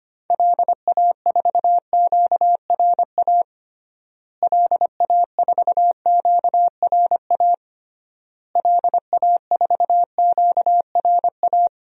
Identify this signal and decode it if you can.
Morse code, heard as a keyed tone, about 25 wpm, 700 Hz. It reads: LA4QRA LA4QRA LA4QRA
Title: DXCC-100 Call Signs - 6 Characters - Repeated 3x - 25wpm